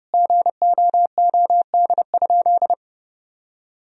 Morse code, heard as a keyed tone, about 32 wpm, 700 Hz. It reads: GOOD?
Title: Sentences from Top 1000 Words - Review Part 4 - 30wpm